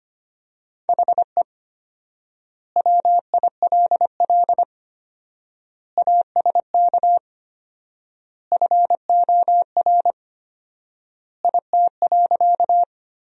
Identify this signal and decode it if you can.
Morse code, heard as a keyed tone, about 25 wpm, 700 Hz. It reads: HE WILL ASK FOR IT.
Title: 4x Word-Spacing - Sentences from Top 200 Words - 25wpm